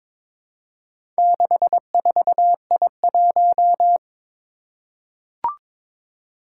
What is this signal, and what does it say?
Morse code, heard as a keyed tone, about 22 wpm, 700 Hz. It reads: 64I1
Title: Four Challenging Letter-Number Combinations 22wpm